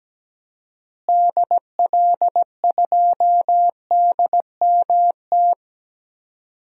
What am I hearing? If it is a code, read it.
Morse code, heard as a keyed tone, about 17 wpm, 700 Hz. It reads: DL2DMT
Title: DXCC-100 Call Signs - 6 Characters - Repeated 2x - 17wpm